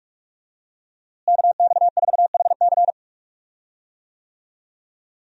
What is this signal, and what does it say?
Morse code, heard as a keyed tone, about 41 wpm, 700 Hz. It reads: KX4HC
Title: USA Call Signs - 5 Characters - Repeated 2x - 45wpm